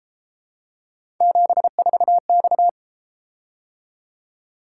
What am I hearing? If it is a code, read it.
Morse code, heard as a keyed tone, about 31 wpm, 700 Hz. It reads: 74X